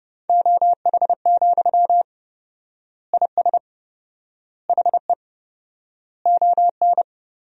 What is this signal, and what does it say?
Morse code, heard as a keyed tone, about 30 wpm, 700 Hz. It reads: OH, IS HE ON
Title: 4x Word-Spacing - Sentences from Top 500 Words - 30wpm